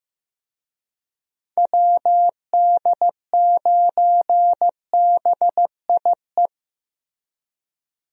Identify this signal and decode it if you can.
Morse code, heard as a keyed tone, about 15 wpm, 700 Hz. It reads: WD9BIE
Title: USA Call Signs - 6 Characters - Repeated 3x - 15wpm